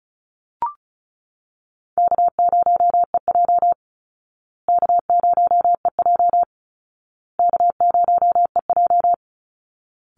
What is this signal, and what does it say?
Morse code, heard as a keyed tone, about 34 wpm, 700 Hz. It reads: K0EJ K0EJ K0EJ
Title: Easy Call Signs Repeated 3x - 35wpm